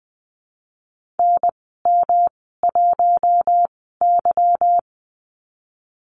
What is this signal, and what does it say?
Morse code, heard as a keyed tone, about 20 wpm, spaced as for 14 wpm, 700 Hz. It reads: NM1Y
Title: 179 Lesson 29g - Slash - Call Signs 20wpm